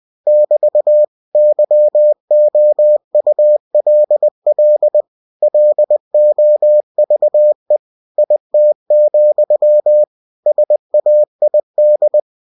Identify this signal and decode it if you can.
Morse code, heard as a keyed tone, about 20 wpm, 600 Hz. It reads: <BT> YOULL LOVE IT, SAID